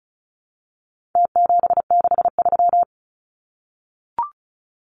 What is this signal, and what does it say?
Morse code, heard as a keyed tone, about 34 wpm, 700 Hz. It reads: T763
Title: Numbers - Contest Style - 4-Digit 35wpm